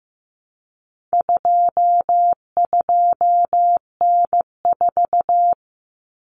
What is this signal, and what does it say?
Morse code, heard as a keyed tone, about 15 wpm, 700 Hz. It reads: 22N4
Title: Common Cut Numbers - 4-Digit 15wpm